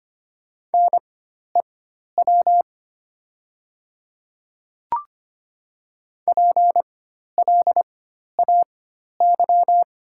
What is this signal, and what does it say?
Morse code, heard as a keyed tone, about 25 wpm, spaced as for 12 wpm, 690 Hz. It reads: NEW PLAY
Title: 4x Character Spacing - Top 200 Words - 25wpm